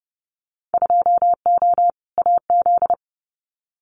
Morse code, heard as a keyed tone, about 30 wpm, 700 Hz. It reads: 2O AZ
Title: Winter Field Day Exchanges 30wpm